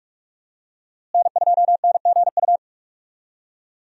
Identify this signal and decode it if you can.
Morse code, heard as a keyed tone, about 46 wpm, 700 Hz. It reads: N2NGU
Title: USA Call Signs - 5 Characters - Repeated 3x - 45wpm